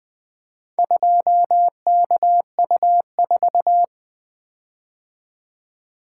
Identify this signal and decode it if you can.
Morse code, heard as a keyed tone, about 24 wpm, 700 Hz. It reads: 2KU4
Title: Speed-Racing - Four Letter-Number Combinations - 20wpm